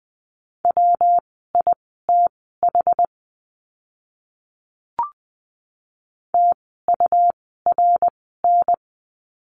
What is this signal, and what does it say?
Morse code, heard as a keyed tone, about 20 wpm, spaced as for 14 wpm, 700 Hz. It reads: WITH TURN
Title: 230 Lesson 37f - Letter x - Words 20wpm